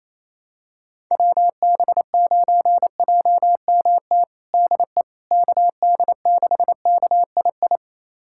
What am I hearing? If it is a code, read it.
Morse code, heard as a keyed tone, about 27 wpm, 700 Hz. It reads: WB9JMT DE KD6KII
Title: USA Callsign DE Callsign - 6 Characters - 28wpm